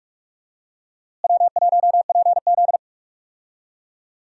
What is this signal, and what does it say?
Morse code, heard as a keyed tone, about 45 wpm, 700 Hz. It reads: W1PZ
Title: USA Call Signs - 4 Characters - Repeated 2x - 45wpm